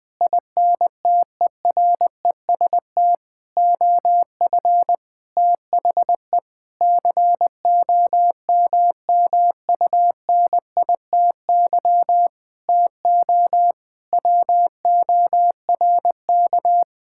Morse code, heard as a keyed tone, about 21 wpm, 700 Hz. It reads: INTEREST OF THE COMMUNITY TO WORK
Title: Sentences with 7-Word Phrases - 20wpm